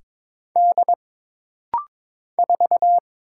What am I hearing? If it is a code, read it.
Morse code, heard as a keyed tone, about 22 wpm, 700 Hz. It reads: D 4
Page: This rapid-fire timing is geared to push your ICR (Instant Character Recognition) skills.